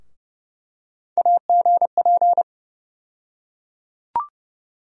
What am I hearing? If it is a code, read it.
Morse code, heard as a keyed tone, about 30 wpm, 700 Hz. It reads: AGP
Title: USA Call Sign Postfixes - 3 Letters 30wpm